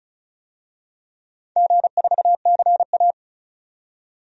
Morse code, heard as a keyed tone, about 36 wpm, 690 Hz. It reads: G4CA
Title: DXCC-100 Call Signs - 4 Characters - 35wpm